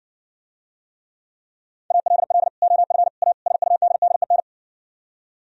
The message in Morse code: ALL OF A SUDDEN

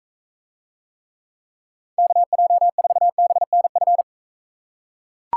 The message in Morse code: KJ4BNF

